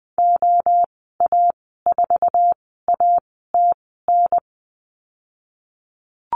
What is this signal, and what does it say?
Morse code, heard as a keyed tone, about 20 wpm, spaced as for 14 wpm, 700 Hz. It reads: OA4ATN
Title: 048 Lesson 9g - Number 4 - Call Signs 20wpm